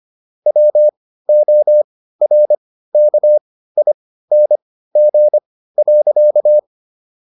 Morse code, heard as a keyed tone, about 25 wpm, spaced as for 15 wpm, 600 Hz. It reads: WORKING.
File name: SS101 15